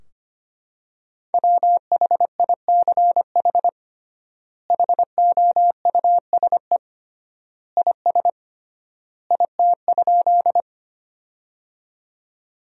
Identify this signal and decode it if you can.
Morse code, heard as a keyed tone, about 25 wpm, 700 Hz. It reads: WHICH HOUSE IS IT?
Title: 3x Word-Spacing - Sentences from Top 200 Words - 25wpm